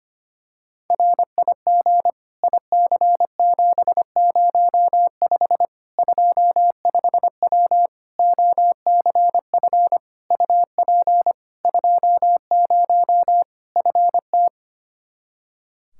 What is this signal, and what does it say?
Morse code, heard as a keyed tone, about 25 wpm, 700 Hz. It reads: RIG IC705 25W OCF UP 20 FT
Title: K1USN SST QSO Practice 25wpm